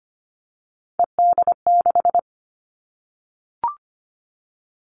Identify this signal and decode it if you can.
Morse code, heard as a keyed tone, about 25 wpm, 700 Hz. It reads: ED6
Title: Callsign Prefixes for Top 100 DXCC - 25wpm